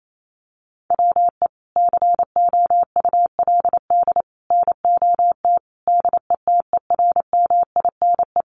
Morse code, heard as a keyed tone, about 28 wpm, 700 Hz. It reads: WE COULD NOT DETERMINE
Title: Sentences from Top 1300 Words 28wpm